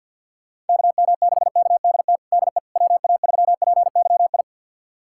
Morse code, heard as a keyed tone, about 52 wpm, 700 Hz. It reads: KM6KDT DE WA3PYI
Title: USA Callsign DE Callsign - 6 Characters - 50wpm